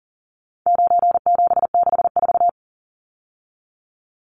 Morse code, heard as a keyed tone, about 40 wpm, 700 Hz. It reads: 9764
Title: Numbers - 4-Digit 40wpm